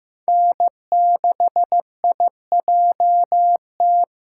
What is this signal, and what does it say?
Morse code, heard as a keyed tone, about 15 wpm, 700 Hz. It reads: N6IJT
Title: SOTA QSO Practice 15wpm